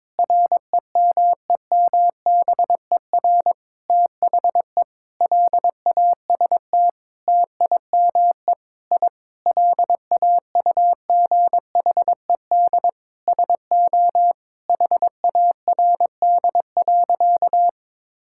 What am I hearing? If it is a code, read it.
Morse code, heard as a keyed tone, about 21 wpm, 700 Hz. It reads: REMEMBER THE LAST TIME I LAUGHED SO HARD.
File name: Sentences with 7-Word Phrases - 22wpm